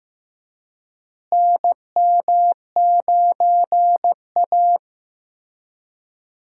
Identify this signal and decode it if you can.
Morse code, heard as a keyed tone, about 16 wpm, 700 Hz. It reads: NM9A
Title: USA Call Signs - 4 Characters - Repeated 3x - 15wpm